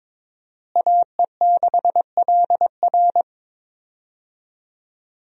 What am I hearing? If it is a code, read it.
Morse code, heard as a keyed tone, about 21 wpm, 700 Hz. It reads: AE6LR
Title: USA Call Signs - 5 Characters - Repeated 2x - 22wpm